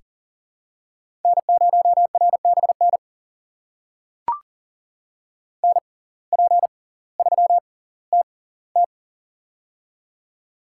Morse code, heard as a keyed tone, about 40 wpm, spaced as for 20 wpm, 700 Hz. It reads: N0RBN NP3TT
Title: USA Call Signs - 5 Characters - 6x Character Spacing - 40wpm